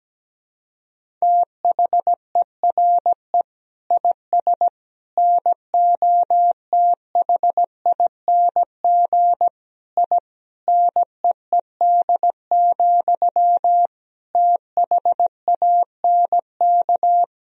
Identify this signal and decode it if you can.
Morse code, heard as a keyed tone, about 17 wpm, 700 Hz. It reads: THERE IS NOTHING I NEED, THANK
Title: Sentences from Top 1000 Words - Review Part 1 - 17wpm